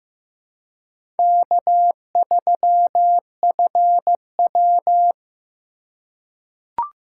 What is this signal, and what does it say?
Morse code, heard as a keyed tone, about 15 wpm, 700 Hz. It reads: K3FW
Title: USA Call Signs - 4 Characters - 4x Character Spacing - 15wpm